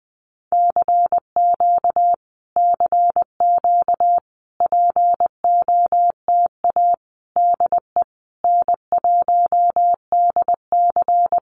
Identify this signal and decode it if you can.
Morse code, heard as a keyed tone, about 20 wpm, 700 Hz. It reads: CQ CQ POTA DE N1DC
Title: Common QSO Phrases Encore 20wpm